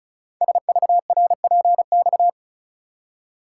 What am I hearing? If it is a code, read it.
Morse code, heard as a keyed tone, about 36 wpm, 700 Hz. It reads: SVRPX
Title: Code Groups - 5 Letters - 35wpm